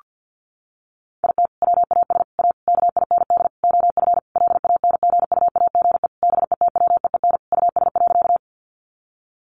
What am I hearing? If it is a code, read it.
Morse code, heard as a keyed tone, about 50 wpm, 700 Hz. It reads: IT WAS A KIND OF LANGUAGE BETWEEN US.
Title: Sentences from Top 500 Words 50wpm